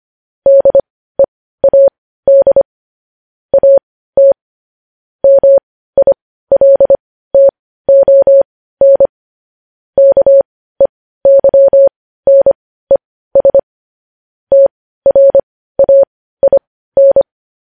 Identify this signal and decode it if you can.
Morse code, heard as a keyed tone, about 25 wpm, spaced as for 15 wpm, 550 Hz. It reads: DEAD AT MILTON KEYNES TRAIN
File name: podcast / News headlines-1x-25x15